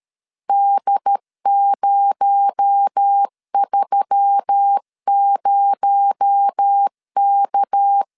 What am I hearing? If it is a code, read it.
Morse code, heard as a keyed tone, about 13 wpm, 800 Hz. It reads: D030K
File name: LEZIONI / 6 PKX0 / 6 PKX0 12wpm 2231